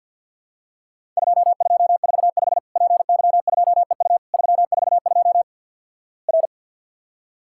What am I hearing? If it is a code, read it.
Morse code, heard as a keyed tone, about 50 wpm, 700 Hz. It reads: 2245 PY2EU 342 R
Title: Worked All Europe DX Contest - QTC Exchange with Repeats - 50wpm